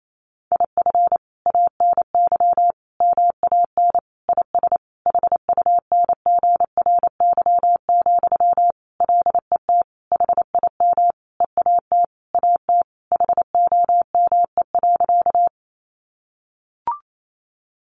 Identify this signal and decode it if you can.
Morse code, heard as a keyed tone, about 28 wpm, 700 Hz. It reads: IF ANY MAN IS HUNGRY, LET HIM EAT AT HOME.
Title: Sentences from Top 1400 Words 28wpm